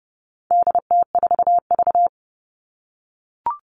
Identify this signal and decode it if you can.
Morse code, heard as a keyed tone, about 30 wpm, 700 Hz. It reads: DT4V
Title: All Cut Numbers - 4-Digit 30wpm